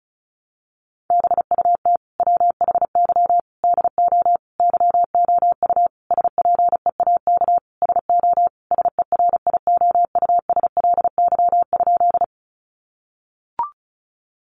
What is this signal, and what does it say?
Morse code, heard as a keyed tone, about 34 wpm, 700 Hz. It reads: BUT WHY DO YOU SPEAK SO SERIOUSLY?
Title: Sentences from Top 1200 Words 35wpm